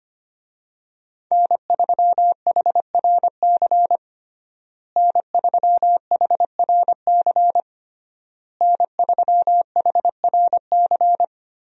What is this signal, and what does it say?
Morse code, heard as a keyed tone, about 25 wpm, 700 Hz. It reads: N3HRC N3HRC N3HRC
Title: USA Call Signs - 5 Characters - Repeated 3x - 25wpm